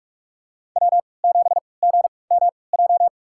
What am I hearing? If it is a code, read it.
Morse code, heard as a keyed tone, about 46 wpm, spaced as for 26 wpm, 700 Hz. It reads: W7GMJ
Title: USA Call Signs - 5 Characters - 3x Character Spacing - 45wpm